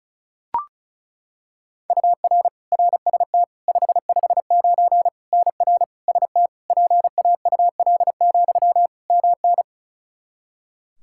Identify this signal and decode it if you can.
Morse code, heard as a keyed tone, about 36 wpm, 700 Hz. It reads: UR RST 559 NR ST PAUL, MN
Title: Practice QSO - Exchange 1 - Beginning - 35wpm